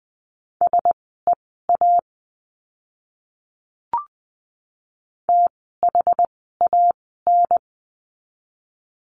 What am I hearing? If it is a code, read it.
Morse code, heard as a keyed tone, about 20 wpm, spaced as for 14 wpm, 700 Hz. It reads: SEA THAN
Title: 061 Lesson 11f - Letter h - Words 20wpm